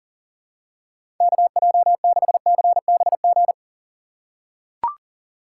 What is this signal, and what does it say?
Morse code, heard as a keyed tone, about 40 wpm, 700 Hz. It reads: KJ6CBG
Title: USA Call Signs - 6 Characters - 2x Character Spacing - 40wpm